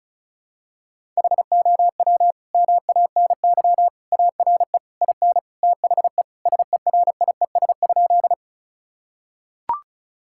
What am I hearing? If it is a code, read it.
Morse code, heard as a keyed tone, about 36 wpm, 690 Hz. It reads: HOW MANY ARE IN THE SERIES?
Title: Sentences from Top 1000 Words - Review Part 5 - 35wpm